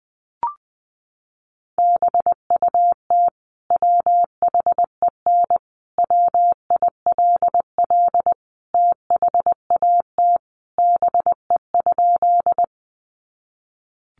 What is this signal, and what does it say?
Morse code, heard as a keyed tone, about 20 wpm, 700 Hz. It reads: BUT WHEN WILL THAT BE?
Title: Sentences from Top 100 Words 20wpm